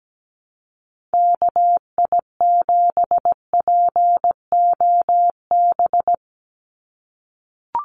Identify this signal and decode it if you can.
Morse code, heard as a keyed tone, about 17 wpm, 700 Hz. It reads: KI7POB
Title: USA Call Signs 17wpm